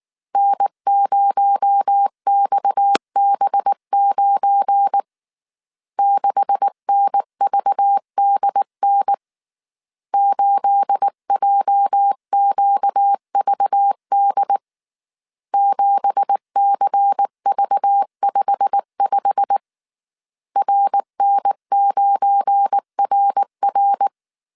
Morse code, heard as a keyed tone, about 19 wpm, 800 Hz. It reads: N0XB9 6NVDN 8JQVD 7CV55 RN9RR